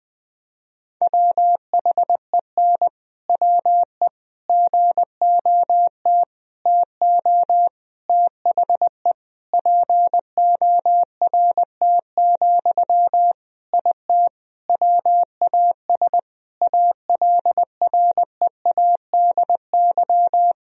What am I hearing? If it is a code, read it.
Morse code, heard as a keyed tone, about 19 wpm, 700 Hz. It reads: WHEN WE GOT TO THE PORT, IT WAS ALREADY
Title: Sentences from Top 1000 Words - Review Part 1 - 20wpm